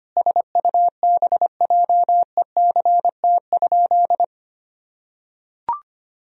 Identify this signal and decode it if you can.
Morse code, heard as a keyed tone, about 25 wpm, 700 Hz. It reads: SUBJECT?
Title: Sentences from Top 1000 Words - Review Part 4 - 25wpm